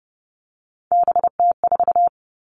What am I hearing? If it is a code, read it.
Morse code, heard as a keyed tone, about 30 wpm, 700 Hz. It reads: BT4
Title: All Cut Numbers - 3-Digit 30wpm